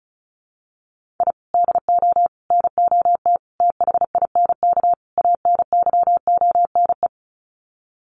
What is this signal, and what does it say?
Morse code, heard as a keyed tone, about 34 wpm, 700 Hz. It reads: I DO NOT THINK ANYONE
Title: Sets of 5 Words Encore 35wpm